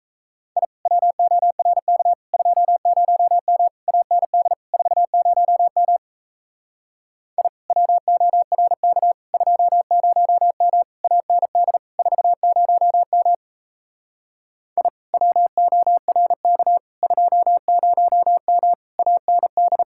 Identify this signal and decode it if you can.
Morse code, heard as a keyed tone, about 37 wpm, 700 Hz. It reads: I WORK 20M AND 40M I WORK 20M AND 40M I WORK 20M AND